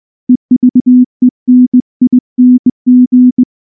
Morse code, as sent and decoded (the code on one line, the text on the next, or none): . ...- . -. .. -. --.
EVENING